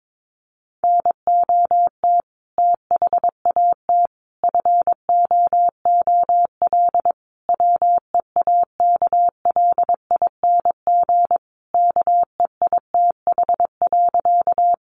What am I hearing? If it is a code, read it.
Morse code, heard as a keyed tone, about 22 wpm, 700 Hz. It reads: NOT THAT FOOL WEAKLING KEITH.
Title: A Sound Of Thunder 22wpm